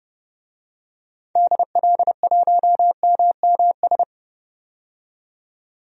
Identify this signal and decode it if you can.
Morse code, heard as a keyed tone, about 30 wpm, 700 Hz. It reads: DL1MMS
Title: DXCC-100 Call Signs - 6 Characters - 30wpm